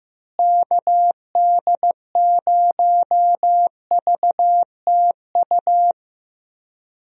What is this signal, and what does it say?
Morse code, heard as a keyed tone, about 15 wpm, 700 Hz. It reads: KD0VTU
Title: USA Call Signs - 6 Characters - Repeated 3x - 15wpm